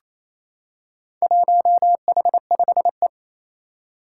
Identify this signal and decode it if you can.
Morse code, heard as a keyed tone, about 27 wpm, 700 Hz. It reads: 1H5E
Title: Four Challenging Letter-Number Combinations 28wpm